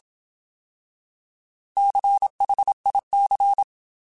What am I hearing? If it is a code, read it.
Morse code, heard as a keyed tone, about 27 wpm, 800 Hz. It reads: CHIC
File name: Alphabetical